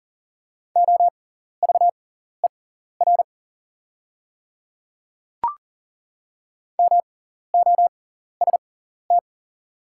Morse code, heard as a keyed tone, about 40 wpm, spaced as for 14 wpm, 690 Hz. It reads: OVER MOST
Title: 6x Character Spacing - Top 100 Words - 40wpm